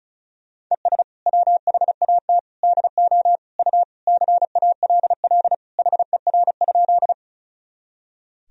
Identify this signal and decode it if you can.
Morse code, heard as a keyed tone, about 36 wpm, 700 Hz. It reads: ES WHAT DO U CALL HER?
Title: Sentences from Top 100 Words and Abbreviations 35wpm